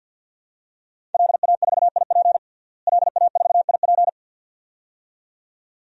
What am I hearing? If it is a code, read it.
Morse code, heard as a keyed tone, about 52 wpm, 700 Hz. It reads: LA4IP LA4IP
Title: DXCC-100 Call Signs - 5 Characters - Repeated 2x - 50wpm